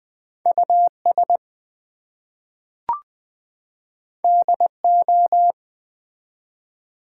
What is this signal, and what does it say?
Morse code, heard as a keyed tone, about 20 wpm, 700 Hz. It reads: US DO
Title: Two Letter Words - 20wpm